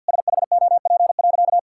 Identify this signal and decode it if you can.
Morse code, heard as a keyed tone, about 50 wpm, 700 Hz. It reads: SHOP.
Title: Sentences from Top 1600 Words 50wpm